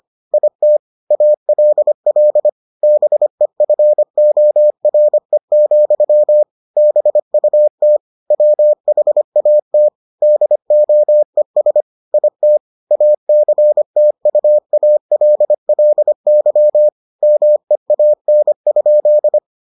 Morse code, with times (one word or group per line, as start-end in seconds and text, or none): 0.3-0.8 IT
1.1-2.5 ALL
2.8-6.4 BEFORE,
6.8-8.0 BUT
8.3-9.9 WHAT
10.2-11.8 DOES
12.1-12.6 IT
12.9-16.9 ACTUALLY
17.2-19.4 MEAN?